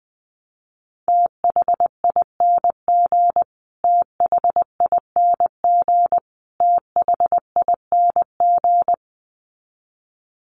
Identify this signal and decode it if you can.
Morse code, heard as a keyed tone, about 20 wpm, 700 Hz. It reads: THING THING THING
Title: Top 500 Words Repeated 20wpm